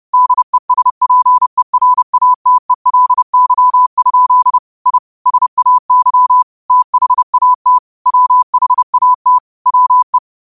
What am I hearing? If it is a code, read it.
Morse code, heard as a keyed tone, about 30 wpm, 1000 Hz. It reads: DESPERATELY? I SAY THAT WHAT WE
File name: qotdfiles / 30wpm